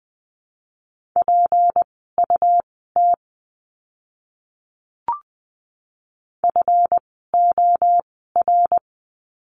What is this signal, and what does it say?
Morse code, heard as a keyed tone, about 20 wpm, spaced as for 15 wpm, 700 Hz. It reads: PUT FOR
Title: 206 Lesson 33f - Letter j - Words 20wpm